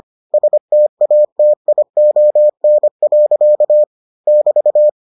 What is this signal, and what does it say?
Morse code, heard as a keyed tone, about 25 wpm, 600 Hz. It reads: STATION. <BT>